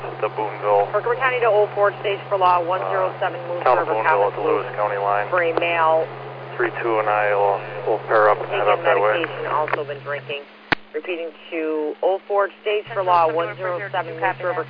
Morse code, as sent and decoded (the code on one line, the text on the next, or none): none